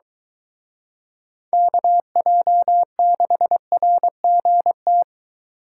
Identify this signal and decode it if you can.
Morse code, heard as a keyed tone, about 23 wpm, 700 Hz. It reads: KJ6RGT